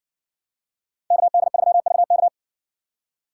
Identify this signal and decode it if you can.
Morse code, heard as a keyed tone, about 60 wpm, 700 Hz. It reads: KD3VK